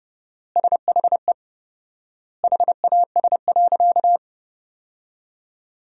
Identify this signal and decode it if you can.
Morse code, heard as a keyed tone, about 30 wpm, 700 Hz. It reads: SHE HAS.
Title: 4x Word-Spacing - Sentences from Top 500 Words - 30wpm